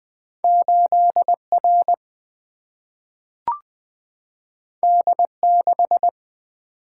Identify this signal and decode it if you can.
Morse code, heard as a keyed tone, about 20 wpm, 700 Hz. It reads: 8 R D 6
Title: Single-Letter Single-Number 20wpm